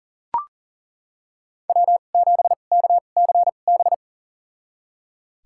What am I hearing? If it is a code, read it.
Morse code, heard as a keyed tone, about 40 wpm, spaced as for 29 wpm, 700 Hz. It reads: W7KCB
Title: USA Call Signs - 5 Characters - 2x Character Spacing - 40wpm